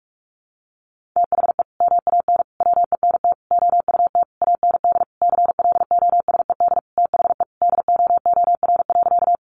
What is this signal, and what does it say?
Morse code, heard as a keyed tone, about 44 wpm, 700 Hz. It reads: THE MAN WENT OUT AND CLOSED THE DOOR.
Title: Sentences from Top 1400 Words 45wpm